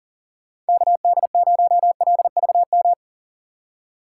Morse code, heard as a keyed tone, about 38 wpm, 700 Hz. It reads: KD0LVM
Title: USA Call Signs - 6 Characters - 4x Character Spacing - 40wpm